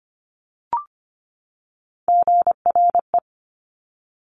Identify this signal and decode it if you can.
Morse code, heard as a keyed tone, about 25 wpm, 700 Hz. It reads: GRE
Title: Top Three Letter Combinations in Words 25wpm